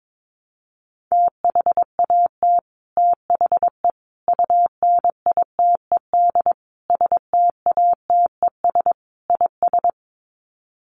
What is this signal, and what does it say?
Morse code, heard as a keyed tone, about 22 wpm, 700 Hz. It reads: THAT THE UNITED STATES IS
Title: Sets of 5 Words Encore 22wpm